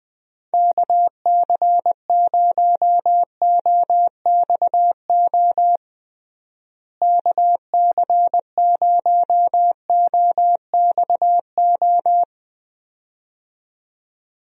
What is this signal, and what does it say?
Morse code, heard as a keyed tone, about 20 wpm, 700 Hz. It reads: KC0OXO KC0OXO
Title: USA Call Signs - 6 Characters - Repeated 2x - 20wpm